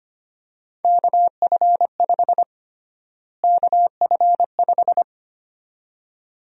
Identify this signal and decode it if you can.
Morse code, heard as a keyed tone, about 25 wpm, 700 Hz. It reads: KF5 KF5
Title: USA Call Sign Prefixes - Repeated 2x - 25wpm